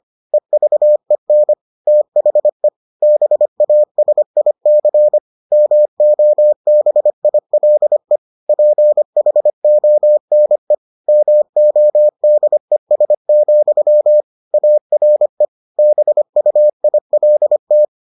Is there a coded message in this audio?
Yes: Morse code from 0.3 to 17.9 s, heard as a keyed tone, about 25 wpm, 600 Hz. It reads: EVEN THE BASIC MOBILE PHONE MODES, ARE BUILT